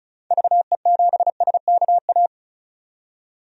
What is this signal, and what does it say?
Morse code, heard as a keyed tone, about 36 wpm, 700 Hz. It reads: VE7SKA
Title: DXCC-100 Call Signs - 6 Characters - 35wpm